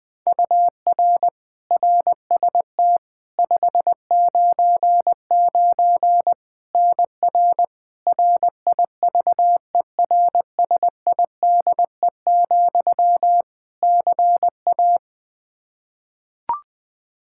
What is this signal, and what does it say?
Morse code, heard as a keyed tone, about 21 wpm, 700 Hz. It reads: UR RST 599 NR RIVERSIDE, CA
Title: Practice QSO - Exchange 1 - Beginning - 20wpm